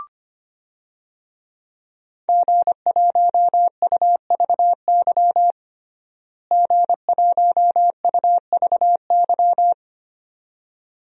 Morse code, heard as a keyed tone, about 25 wpm, 700 Hz. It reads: G1UVY G1UVY
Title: DXCC-100 Call Signs - 5 Characters - Repeated 2x - 25wpm